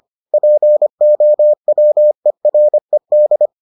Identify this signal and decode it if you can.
Morse code, heard as a keyed tone, about 25 wpm, 600 Hz. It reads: POWERED